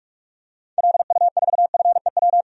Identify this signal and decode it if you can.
Morse code, heard as a keyed tone, about 46 wpm, 700 Hz. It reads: LU4FEW
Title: DXCC-100 Call Signs - 6 Characters - 45wpm